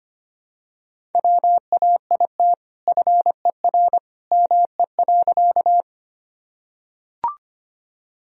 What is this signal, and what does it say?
Morse code, heard as a keyed tone, about 25 wpm, 700 Hz. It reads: WAIT FER ME.
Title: Sentences from Top 500 Words and Abbreviations 25wpm